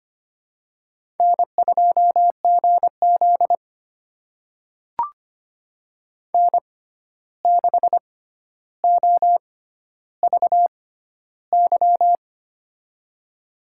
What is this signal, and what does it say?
Morse code, heard as a keyed tone, about 25 wpm, spaced as for 12 wpm, 700 Hz. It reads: N2GZ N6OVY
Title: USA Call Signs - 6x Character Spacing - 25wpm